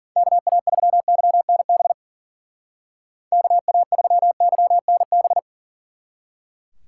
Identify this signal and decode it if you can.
Morse code, heard as a keyed tone, about 48 wpm, 700 Hz. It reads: KA3YNB KA3YNB
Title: Speed Racing - USA 2x3 Call Signs 40wpm